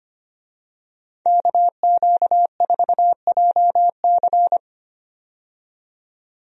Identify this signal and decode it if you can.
Morse code, heard as a keyed tone, about 25 wpm, 690 Hz. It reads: KQ4JC
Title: USA Call Signs - 5 Characters - 25wpm